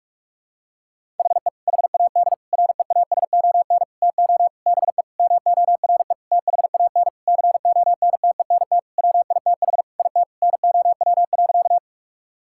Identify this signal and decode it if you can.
Morse code, heard as a keyed tone, about 44 wpm, 700 Hz. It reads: HE HAD REASON TO BE MORE THAN CONTENT WITH IT NOW.
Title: Sentences from Top 1000 Words - Review Part 6 - 45wpm